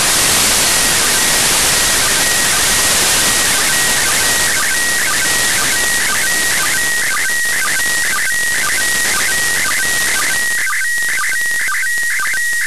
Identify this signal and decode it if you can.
SSTV (slow-scan television), PD 120.
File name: practicas-en-abierto / decodifica-imagenes-iss